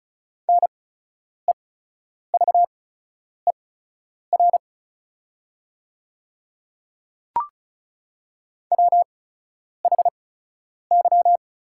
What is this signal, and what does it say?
Morse code, heard as a keyed tone, about 36 wpm, spaced as for 10 wpm, 700 Hz. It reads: NEVER WHY
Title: 8x Character Spacing - Top 300 Words - 35wpm